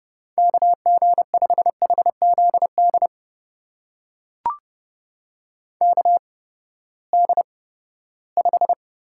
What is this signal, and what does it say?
Morse code, heard as a keyed tone, about 30 wpm, spaced as for 17 wpm, 700 Hz. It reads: KG5HZD KD5
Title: USA Call Signs - 6 Characters - 8x Character Spacing - 30wpm